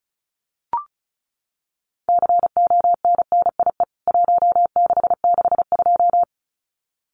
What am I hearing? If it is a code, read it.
Morse code, heard as a keyed tone, about 34 wpm, 700 Hz. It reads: CONNIE 1662
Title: CWT Exchange 35wpm